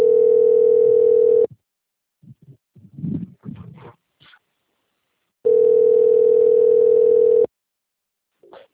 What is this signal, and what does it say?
Ringback tone (North America), 440 + 480 Hz.